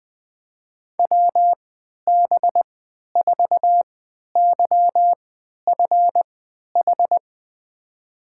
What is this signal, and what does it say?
Morse code, heard as a keyed tone, about 21 wpm, spaced as for 11 wpm, 700 Hz. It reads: WB4YFH
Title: USA Call Signs - 6 Characters - 3x Character Spacing - 20wpm